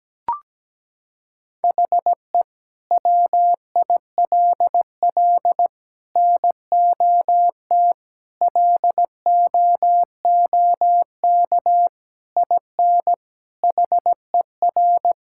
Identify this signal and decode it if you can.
Morse code, heard as a keyed tone, about 17 wpm, 700 Hz. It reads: HE WILL NOT LOOK IN HER
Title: Sentences from Top 1000 Words - Review Part 5 - 17wpm